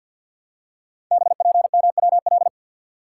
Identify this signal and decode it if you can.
Morse code, heard as a keyed tone, about 49 wpm, 700 Hz. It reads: BPMWL
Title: Code Groups - 5 Letters - 50wpm